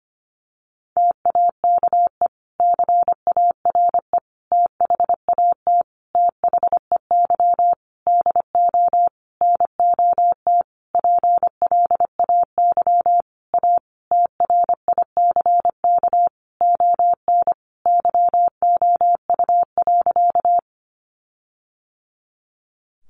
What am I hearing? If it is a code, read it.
Morse code, heard as a keyed tone, about 25 wpm, 700 Hz. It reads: TAKE CARE THAT THEY DO NOT PLAY A TRICK ON YOU.
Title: Sentences from Top 1500 Words 25wpm